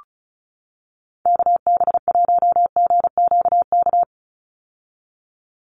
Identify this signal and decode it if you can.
Morse code, heard as a keyed tone, about 34 wpm, 700 Hz. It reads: KB1GQK
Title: USA Call Signs 35wpm